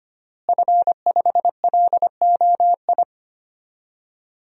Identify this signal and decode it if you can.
Morse code, heard as a keyed tone, about 25 wpm, 700 Hz. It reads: F5LOI
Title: DXCC-100 Call Signs - 5 Characters - Repeated 3x - 25wpm